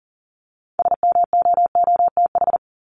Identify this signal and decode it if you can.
Morse code, heard as a keyed tone, about 40 wpm, 700 Hz. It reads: SMOOTH
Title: Top 1251-1500 Words 40wpm